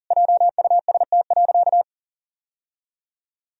Morse code, heard as a keyed tone, about 40 wpm, 700 Hz. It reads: JUST.
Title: Sentences from Top 1000 Words - Review Part 1 - 40wpm